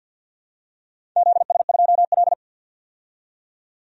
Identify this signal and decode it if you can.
Morse code, heard as a keyed tone, about 53 wpm, 700 Hz. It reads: ZS2L
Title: DXCC-100 Call Signs - 4 Characters - 50wpm